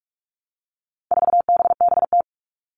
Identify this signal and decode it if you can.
Morse code, heard as a keyed tone, about 44 wpm, 700 Hz. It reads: 4BBT
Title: All Cut Numbers - 4-Digit 45wpm